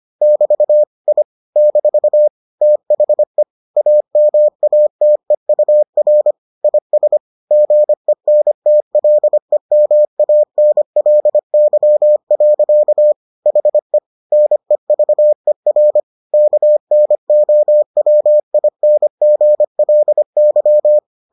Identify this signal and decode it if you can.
Morse code, heard as a keyed tone, about 25 wpm, 600 Hz. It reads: <BT> I - THE AMATEUR IS GENTLEMANLY. HE NEVER KNOWINGLY